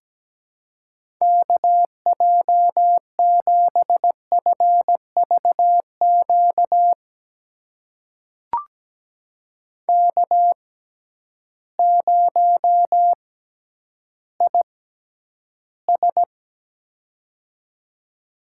Morse code, heard as a keyed tone, about 17 wpm, spaced as for 10 wpm, 700 Hz. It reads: KJ7FVQ K0IS
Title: USA Call Signs - 6x Character Spacing - 17wpm